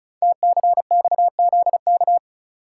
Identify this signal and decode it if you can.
Morse code, heard as a keyed tone, about 36 wpm, 700 Hz. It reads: TCXZK
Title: Code Groups - 5 Letters - 35wpm